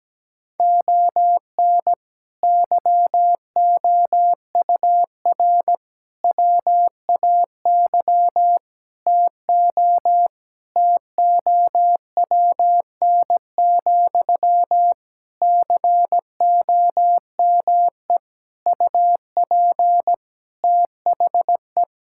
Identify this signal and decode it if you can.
Morse code, heard as a keyed tone, about 17 wpm, 700 Hz. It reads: ON YOUR WAY TO TOWN, COME UP THE
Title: Sentences from Top 1000 Words - Review Part 6 - 17wpm